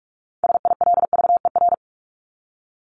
Morse code, heard as a keyed tone, about 44 wpm, 700 Hz. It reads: SILVER
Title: Top 1751-2000 Words 45wpm